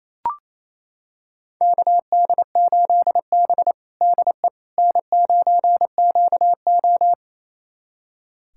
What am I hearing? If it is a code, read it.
Morse code, heard as a keyed tone, about 27 wpm, 700 Hz. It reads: KD8B DE N9QO
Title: USA Callsign DE Callsign - 4 Characters - 28wpm